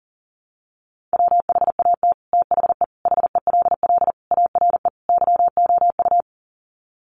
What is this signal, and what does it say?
Morse code, heard as a keyed tone, about 40 wpm, 700 Hz. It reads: WHAT THE HELL ARE YOU
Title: Sets of 5 Words Encore 40wpm